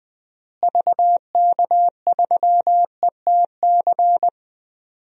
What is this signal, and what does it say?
Morse code, heard as a keyed tone, about 20 wpm, 700 Hz. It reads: VK3ETC
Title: DXCC-100 Call Signs - 6 Characters - Repeated 3x - 20wpm